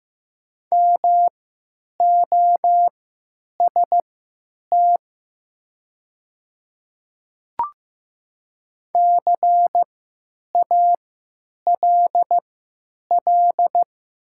Morse code, heard as a keyed tone, about 15 wpm, spaced as for 9 wpm, 700 Hz. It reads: MOST CALL
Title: 3x Character Spacing - Top 100 Words - 15wpm